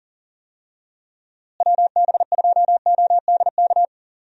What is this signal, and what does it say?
Morse code, heard as a keyed tone, about 40 wpm, 700 Hz. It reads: WB2ODK
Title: USA Call Signs - 6 Characters - Repeated 2x - 40wpm